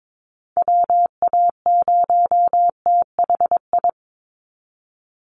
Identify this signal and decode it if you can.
Morse code, heard as a keyed tone, about 22 wpm, 700 Hz. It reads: WA0THI
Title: USA Call Signs 22wpm